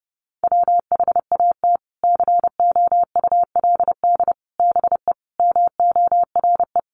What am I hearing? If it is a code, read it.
Morse code, heard as a keyed tone, about 30 wpm, 700 Hz. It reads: WHAT COULD BE MORE
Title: Sentences from Top 1000 Words 30wpm